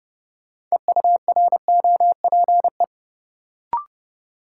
Morse code, heard as a keyed tone, about 30 wpm, 700 Hz. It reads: EUROPE
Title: Sets of 5 Words Encore - New Word Drill - 30wpm